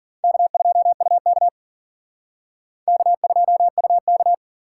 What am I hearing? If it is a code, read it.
Morse code, heard as a keyed tone, about 46 wpm, 700 Hz. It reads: K2UK K2UK
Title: Speed Racing - USA 1x2 Call Signs 40wpm